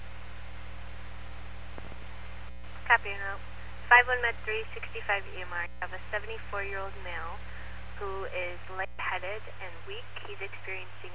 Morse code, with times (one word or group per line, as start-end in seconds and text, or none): none